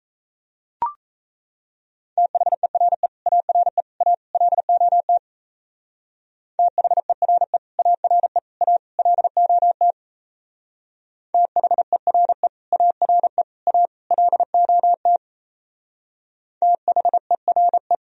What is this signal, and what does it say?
Morse code, heard as a keyed tone, about 32 wpm, 700 Hz. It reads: THERE ARE A LOT THERE ARE A LOT THERE ARE A LOT THERE
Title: Speed-Racing - Sets of 4 Words 28wpm